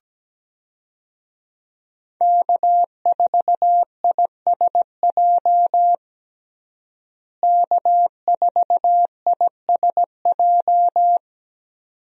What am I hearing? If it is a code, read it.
Morse code, heard as a keyed tone, about 17 wpm, 700 Hz. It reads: K4ISJ K4ISJ
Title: USA Call Signs - 5 Characters - Repeated 2x - 17wpm